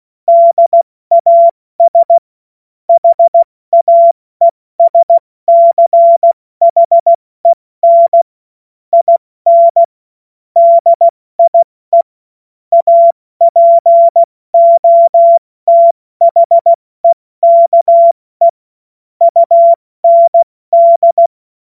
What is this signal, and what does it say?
Morse code, heard as a keyed tone, about 16 wpm, 680 Hz. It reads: DAS HAESCHEN IN DIE APOTHEKE UND
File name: Morsekurs / Witze / Witz 25. Mai 2021, 70 F80